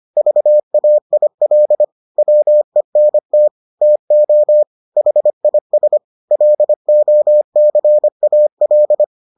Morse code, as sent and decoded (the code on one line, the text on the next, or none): ...- .- .. .-.. / .-- . -. - / - --- / .... .. ... / .-.. --- -.-. .- .-..
VAIL WENT TO HIS LOCAL